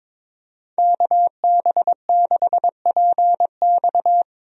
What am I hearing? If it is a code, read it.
Morse code, heard as a keyed tone, about 21 wpm, 700 Hz. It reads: KB6PX
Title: USA Call Signs - 5 Characters - Repeated 3x - 22wpm